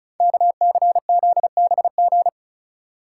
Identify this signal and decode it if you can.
Morse code, heard as a keyed tone, about 36 wpm, 690 Hz. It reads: KCZBG
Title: Code Groups - 5 Letters - 35wpm